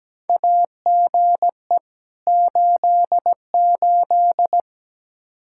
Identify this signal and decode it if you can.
Morse code, heard as a keyed tone, about 17 wpm, 700 Hz. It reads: AGE 88
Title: Practice QSO - Exchange 3 - More Info - 17wpm